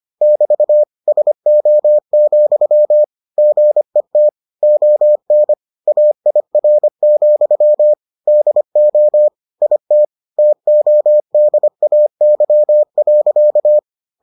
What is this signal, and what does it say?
Morse code, heard as a keyed tone, about 25 wpm, 600 Hz. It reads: <BT> SO, GET ON AIR, DO IT TODAY.